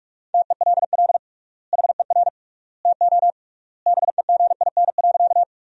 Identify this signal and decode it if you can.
Morse code, heard as a keyed tone, about 46 wpm, 700 Hz. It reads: TELL HER TO BEGIN.
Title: 3x Word-Spacing - Sentences from Top 300 Words - 45wpm